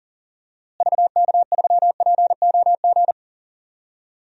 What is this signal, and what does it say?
Morse code, heard as a keyed tone, about 40 wpm, 700 Hz. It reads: VK3POG
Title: DXCC-100 Call Signs - 6 Characters - Repeated 2x - 40wpm